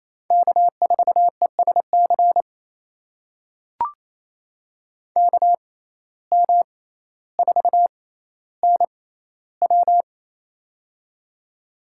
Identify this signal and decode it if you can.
Morse code, heard as a keyed tone, about 28 wpm, spaced as for 14 wpm, 700 Hz. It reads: K4ESC KM4NW
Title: USA Call Signs - 5 Characters - 6x Character Spacing - 28wpm